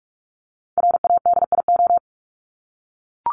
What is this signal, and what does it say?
Morse code, heard as a keyed tone, about 44 wpm, 700 Hz. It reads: RADIO
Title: Top 501-750 Words 45wpm